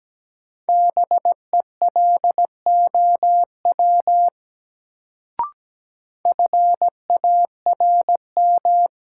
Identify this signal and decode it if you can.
Morse code, heard as a keyed tone, about 17 wpm, 700 Hz. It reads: BELOW FARM
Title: Sentences from Top 1200 Words - New Word Drill Alt - 17wpm